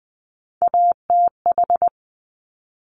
Morse code, heard as a keyed tone, about 20 wpm, 700 Hz. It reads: ATH
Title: Top Three Letter Combinations in Words 20wpm